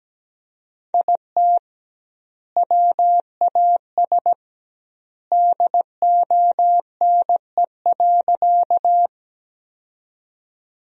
Morse code, heard as a keyed tone, about 17 wpm, 700 Hz. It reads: IT WAS DONE.
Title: 2x Word-Spacing - Sentences from Top 500 Words - 17wpm